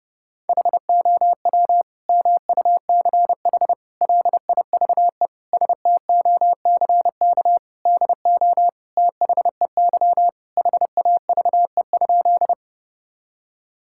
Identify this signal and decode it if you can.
Morse code, heard as a keyed tone, about 30 wpm, 700 Hz. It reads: HOW MUCH LIVE STOCK DO THEY HAVE?
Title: Sentences from Top 1000 Words - Review Part 5 - 30wpm